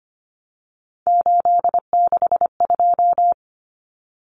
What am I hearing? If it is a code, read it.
Morse code, heard as a keyed tone, about 25 wpm, 700 Hz. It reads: 862
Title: Common Cut Numbers - 3-Digit 25wpm